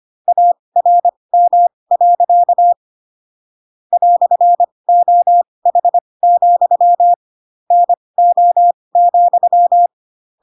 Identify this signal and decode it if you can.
Morse code, heard as a keyed tone, about 25 wpm, spaced as for 20 wpm, 700 Hz. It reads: ARM. "OH, NO,
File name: Pinocchio / Pinocchio ch7 15 25 ik1hge